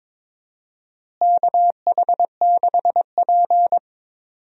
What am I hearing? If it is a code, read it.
Morse code, heard as a keyed tone, about 22 wpm, 700 Hz. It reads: KH6P
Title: DXCC-100 Call Signs - 4 Characters - Repeated 3x - 22wpm